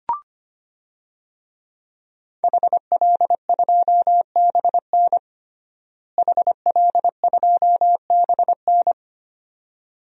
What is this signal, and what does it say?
Morse code, heard as a keyed tone, about 25 wpm, 700 Hz. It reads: HL2BN HL2BN
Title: DXCC-100 Call Signs - 5 Characters - Repeated 2x - 25wpm